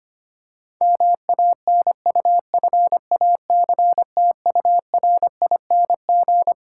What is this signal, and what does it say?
Morse code, heard as a keyed tone, about 25 wpm, 700 Hz. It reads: MANUFACTURING
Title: Practice QSO - Exchange 3 - More Info - 25wpm